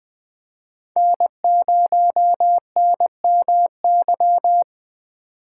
Morse code, heard as a keyed tone, about 20 wpm, 700 Hz. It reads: N0NMY
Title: USA Call Signs - 6x Character Spacing - 20wpm